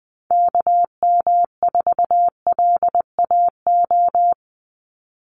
Morse code, heard as a keyed tone, about 20 wpm, 700 Hz. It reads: KM4LAO
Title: Call Signs Repeated 3x - 20wpm